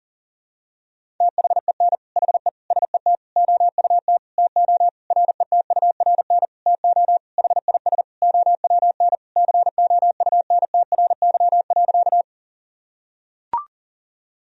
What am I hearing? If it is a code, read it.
Morse code, heard as a keyed tone, about 40 wpm, 700 Hz. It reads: THEN HE SET OUT TO RETURN TO HIS OWN COUNTRY.
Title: Sentences from Top 1000 Words - Review Part 1 - 40wpm